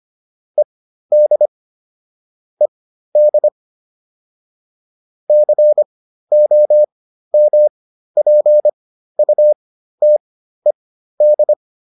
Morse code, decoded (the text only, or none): ED ED COMPUTED